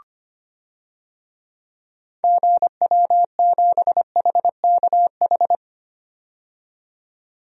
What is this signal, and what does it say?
Morse code, heard as a keyed tone, about 25 wpm, 700 Hz. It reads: GW7HKH
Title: DXCC-100 Call Signs - 6 Characters - 25wpm